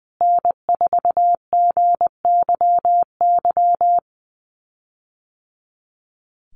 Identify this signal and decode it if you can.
Morse code, heard as a keyed tone, about 19 wpm, 700 Hz. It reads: N4GYY
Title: USA Call Signs Repeated 2x 20wpm